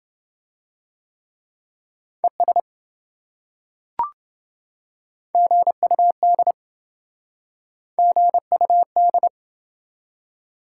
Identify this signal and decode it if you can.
Morse code, heard as a keyed tone, about 27 wpm, 700 Hz. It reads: ES GUD GUD